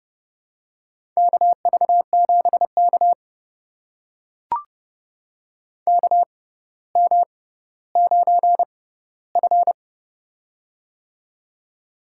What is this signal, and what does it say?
Morse code, heard as a keyed tone, about 32 wpm, spaced as for 16 wpm, 700 Hz. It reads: KV7K KM9F
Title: USA Call Signs - 4 Characters - 6x Character Spacing - 30wpm